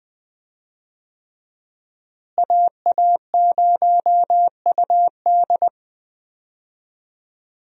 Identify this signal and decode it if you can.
Morse code, heard as a keyed tone, about 20 wpm, 700 Hz. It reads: AA0UD
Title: USA Call Signs - 5 Characters - Repeated 3x - 20wpm